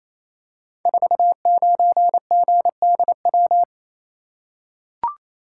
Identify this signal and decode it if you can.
Morse code, heard as a keyed tone, about 29 wpm, 700 Hz. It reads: 49GDW
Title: Code Groups - 5 Letter:Numbers - 28wpm